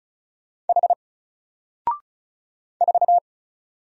Morse code, heard as a keyed tone, about 34 wpm, 700 Hz. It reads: H 4
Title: Single Letter-Number Alt 35wpm